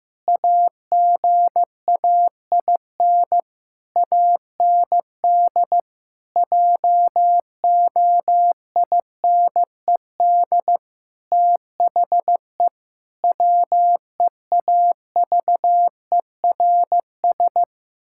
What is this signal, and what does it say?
Morse code, heard as a keyed tone, about 16 wpm, 700 Hz. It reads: AGAIN AND JOINED THE WEAVERS
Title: Pangram Challenge v2 - 15wpm